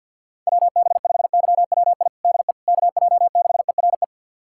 Morse code, heard as a keyed tone, about 52 wpm, 700 Hz. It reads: WB5YWI DE KJ6ERE
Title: USA Callsign DE Callsign - 6 Characters - 50wpm